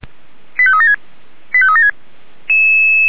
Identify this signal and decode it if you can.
Five-tone selective call (ZVEI), heard as quick runs of five tones, tones of about 70 ms. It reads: 85376 85376